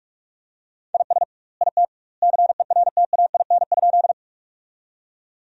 Morse code, heard as a keyed tone, about 42 wpm, 700 Hz. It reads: IS IT CERTAIN?
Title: 2x Word-Spacing - Sentences from Top 500 Words - 45wpm